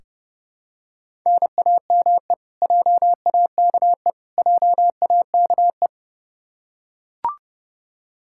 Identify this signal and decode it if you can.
Morse code, heard as a keyed tone, about 30 wpm, 700 Hz. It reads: NAME JAKE JAKE
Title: Practice QSO - Exchange 1 - Beginning - 30wpm